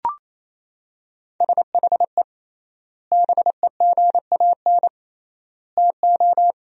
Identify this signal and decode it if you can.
Morse code, heard as a keyed tone, about 28 wpm, 700 Hz. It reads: SHE BEGAN TO
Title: 3x Word-Spacing - Sentences from Top 500 Words - 28wpm